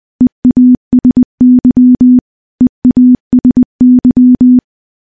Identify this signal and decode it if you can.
Morse code, heard as a keyed tone, about 20 wpm, 250 Hz. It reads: EASY EASY
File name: morse-training-20WPM